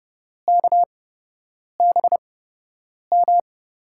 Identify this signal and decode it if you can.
Morse code, heard as a keyed tone, about 32 wpm, spaced as for 8 wpm, 700 Hz. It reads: KBM
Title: USA Call Signs - 6 Characters - 8x Character Spacing - 30wpm